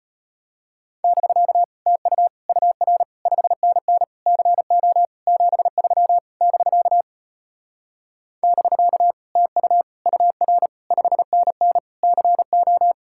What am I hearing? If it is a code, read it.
Morse code, heard as a keyed tone, about 36 wpm, 700 Hz. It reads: <BK> TU UR 5NN CO 73 <BK> <BK> TU UR 5NN CO